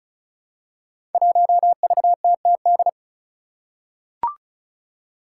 Morse code, heard as a keyed tone, about 34 wpm, 700 Hz. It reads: 1VTTD
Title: Five Challenging Letter-Number Combinations 35wpm